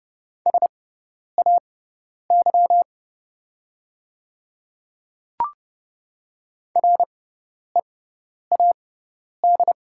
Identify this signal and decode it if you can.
Morse code, heard as a keyed tone, about 30 wpm, spaced as for 10 wpm, 700 Hz. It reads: SAY READ
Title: 6x Character Spacing - Top 200 Words - 30wpm